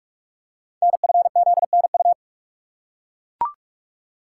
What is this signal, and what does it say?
Morse code, heard as a keyed tone, about 44 wpm, 700 Hz. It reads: NFZNU
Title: Code Groups - 5 Letters - 45wpm